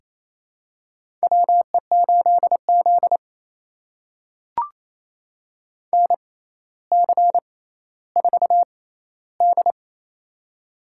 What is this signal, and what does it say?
Morse code, heard as a keyed tone, about 28 wpm, spaced as for 14 wpm, 700 Hz. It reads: WE8Z NC4D
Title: USA Call Signs - 6x Character Spacing - 28wpm